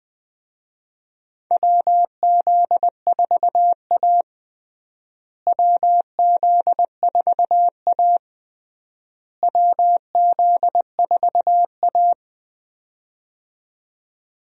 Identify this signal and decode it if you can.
Morse code, heard as a keyed tone, about 20 wpm, 700 Hz. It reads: WZ4A WZ4A WZ4A
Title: USA Call Signs - 4 Characters - Repeated 3x - 20wpm